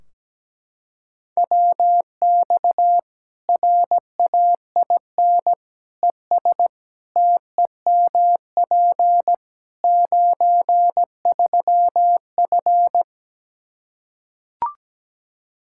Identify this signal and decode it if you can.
Morse code, heard as a keyed tone, about 17 wpm, 700 Hz. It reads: WX RAIN ES TEMP 93F
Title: Practice QSO - Exchange 2 - Info - 17wpm